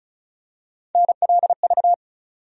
Morse code, heard as a keyed tone, about 34 wpm, 700 Hz. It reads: NLV
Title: USA Call Sign Postfixes - 3 Letters 35wpm